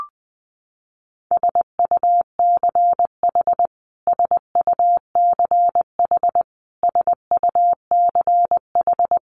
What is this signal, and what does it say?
Morse code, heard as a keyed tone, about 20 wpm, 700 Hz. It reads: SUCH SUCH SUCH
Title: Top 500 Words Repeated 20wpm